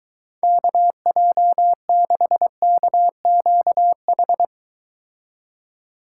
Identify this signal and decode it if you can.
Morse code, heard as a keyed tone, about 23 wpm, 700 Hz. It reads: KJ6KQH